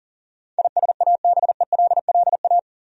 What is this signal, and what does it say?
Morse code, heard as a keyed tone, about 37 wpm, 700 Hz. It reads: ISABELLA
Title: USA Common Names - 40wpm